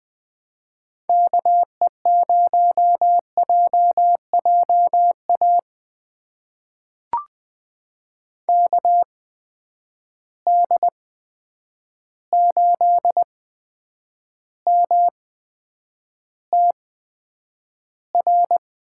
Morse code, heard as a keyed tone, about 20 wpm, spaced as for 9 wpm, 700 Hz. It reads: KE0JJA KD8MTR
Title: USA Call Signs - 8x Character Spacing - 20wpm